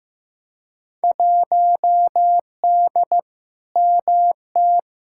Morse code, heard as a keyed tone, about 15 wpm, 700 Hz. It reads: 1D MT
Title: Field Day Exchanges - 15wpm